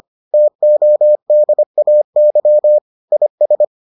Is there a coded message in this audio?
Yes: Morse code from 0.3 to 3.6 s, heard as a keyed tone, about 25 wpm, 600 Hz. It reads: TODAY IS